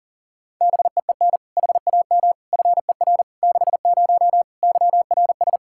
Morse code, heard as a keyed tone, about 37 wpm, 700 Hz. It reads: BEEN HAM FER 60 YRS
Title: K1USN SST QSO Practice 40wpm